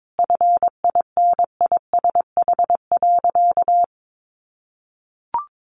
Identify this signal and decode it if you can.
Morse code, heard as a keyed tone, about 22 wpm, 700 Hz. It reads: FINISH.
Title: Sentences from Top 1200 Words 22wpm